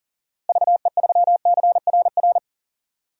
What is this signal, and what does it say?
Morse code, heard as a keyed tone, about 40 wpm, 700 Hz. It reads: VE3CRR
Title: DXCC-100 Call Signs - 6 Characters - Repeated 3x - 40wpm